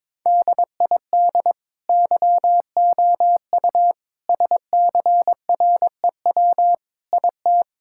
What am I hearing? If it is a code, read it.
Morse code, heard as a keyed tone, about 21 wpm, 700 Hz. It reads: DID YOU SCREW IT
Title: Sentences from Top 1600 Words 22wpm